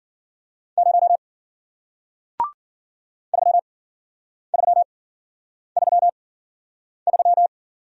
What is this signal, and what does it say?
Morse code, heard as a keyed tone, about 53 wpm, spaced as for 9 wpm, 700 Hz. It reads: 0 3333